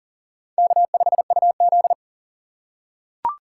Morse code, heard as a keyed tone, about 40 wpm, 700 Hz. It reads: K5UZ
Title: USA Call Signs - 4 Characters - 6x Character Spacing - 40wpm